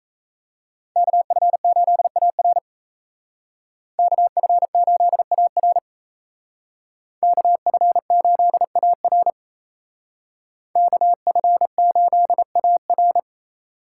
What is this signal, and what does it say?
Morse code, heard as a keyed tone, about 33 wpm, 700 Hz. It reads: KF8AR KF8AR KF8AR KF8AR
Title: Speed Racing - USA Call Signs 28wpm